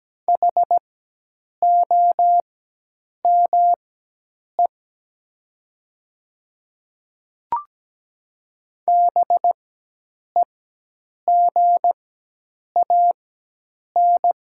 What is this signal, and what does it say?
Morse code, heard as a keyed tone, about 17 wpm, spaced as for 8 wpm, 700 Hz. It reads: HOME BEGAN
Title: 4x Character Spacing - Top 300 Words - 17wpm